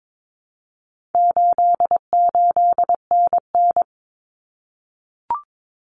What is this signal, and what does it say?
Morse code, heard as a keyed tone, about 22 wpm, 700 Hz. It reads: 88NN
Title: Common Cut Numbers - 4-Digit 22wpm